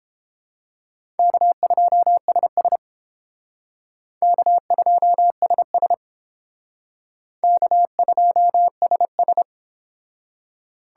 Morse code, heard as a keyed tone, about 31 wpm, 700 Hz. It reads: K2SS K2SS K2SS